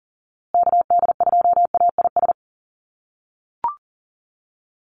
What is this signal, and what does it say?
Morse code, heard as a keyed tone, about 40 wpm, 700 Hz. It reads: KD2AIS
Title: USA Call Signs Repeated 2x 40wpm